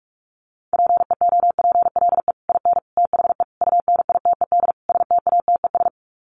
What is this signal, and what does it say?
Morse code, heard as a keyed tone, about 44 wpm, 700 Hz. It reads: PEOPLE IN THE UNITED STATES
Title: Sets of 5 Words Encore 45wpm